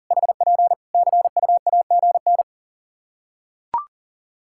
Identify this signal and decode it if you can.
Morse code, heard as a keyed tone, about 40 wpm, 700 Hz. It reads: HP CUAGN
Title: Practice QSO - Exchange 4 - Ending - 40wpm